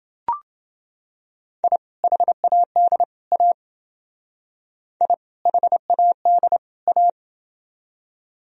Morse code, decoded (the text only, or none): I HAD A I HAD A